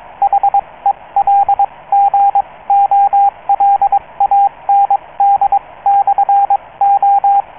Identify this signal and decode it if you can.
Morse code, heard as a keyed tone, about 22 wpm, spaced as for 18 wpm, 800 Hz, with radio noise. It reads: HELGOLAND/O